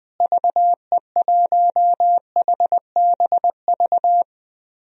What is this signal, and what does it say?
Morse code, heard as a keyed tone, about 20 wpm, 700 Hz. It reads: VE1HBV
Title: DXCC-100 Call Signs - 6 Characters - Repeated 3x - 20wpm